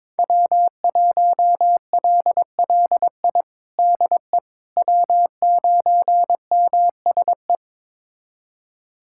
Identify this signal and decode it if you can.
Morse code, heard as a keyed tone, about 21 wpm, 700 Hz. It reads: W1LLI DE W9MSE
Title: USA Callsign DE Callsign - 5 Characters - 22wpm